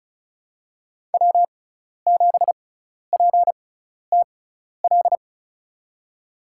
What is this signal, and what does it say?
Morse code, heard as a keyed tone, about 36 wpm, spaced as for 12 wpm, 700 Hz. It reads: W7PTL
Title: USA Call Signs - 5 Characters - 6x Character Spacing - 35wpm